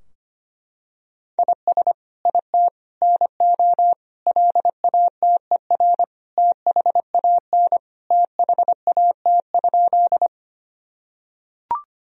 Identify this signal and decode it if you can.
Morse code, heard as a keyed tone, about 25 wpm, 700 Hz. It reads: IS IT NO LATER THAN THAT?
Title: Sentences from Top 1000 Words - Review Part 4 - 25wpm